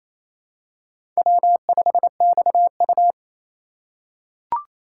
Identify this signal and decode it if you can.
Morse code, heard as a keyed tone, about 28 wpm, 700 Hz. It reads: W5XU
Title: K1USN SST QSO Practice - 28wpm